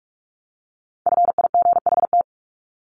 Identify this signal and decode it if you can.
Morse code, heard as a keyed tone, about 47 wpm, 700 Hz. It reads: FIGHT
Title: Top 1501-1750 Words 45wpm